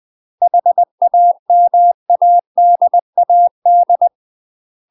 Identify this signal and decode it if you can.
Morse code, heard as a keyed tone, about 20 wpm, 700 Hz. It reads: HAMADAD
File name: UK MASTER QSOs TRAINING / 20WPM / UK MASTER QSOs 0028